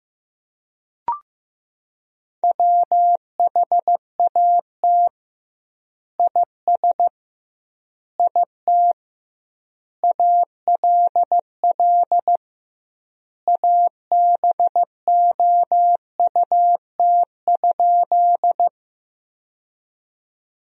Morse code, heard as a keyed tone, about 16 wpm, 700 Hz. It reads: WHAT IS IT ALL ABOUT?
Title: 2x Word-Spacing - Sentences from Top 100 Words - 15wpm